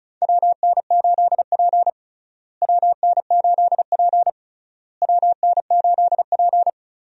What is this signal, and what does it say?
Morse code, heard as a keyed tone, about 36 wpm, 700 Hz. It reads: WN8P WN8P WN8P
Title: USA Call Signs - 4 Characters - Repeated 3x - 35wpm